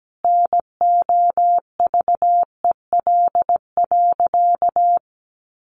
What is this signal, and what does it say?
Morse code, heard as a keyed tone, about 17 wpm, 700 Hz. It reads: NOVEL.
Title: Sentences from Top 1300 Words 17wpm